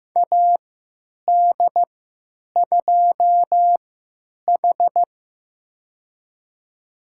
Morse code, heard as a keyed tone, about 15 wpm, spaced as for 8 wpm, 700 Hz. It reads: AD2H
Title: USA Call Signs - 4 Characters - 3x Character Spacing - 15wpm